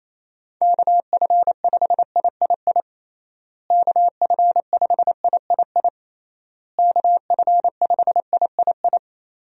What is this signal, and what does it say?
Morse code, heard as a keyed tone, about 28 wpm, 700 Hz. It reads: KF5III KF5III KF5III
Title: USA Call Signs - 6 Characters - Repeated 3x - 28wpm